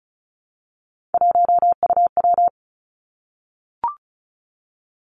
Morse code, heard as a keyed tone, about 33 wpm, 700 Hz. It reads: 1UW
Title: Three Challenging Letter-Number Combinations 35wpm